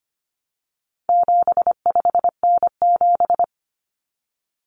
Morse code, heard as a keyed tone, about 25 wpm, 700 Hz. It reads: 75N7
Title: Common Cut Numbers - 4-Digit 25wpm